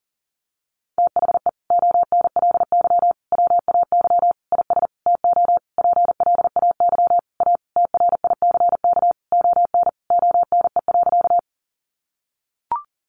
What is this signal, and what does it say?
Morse code, heard as a keyed tone, about 37 wpm, 700 Hz. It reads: THE ONLY WAY IS TO PLAY A TRICK ON ONE.
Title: Sentences from Top 1500 Words 40wpm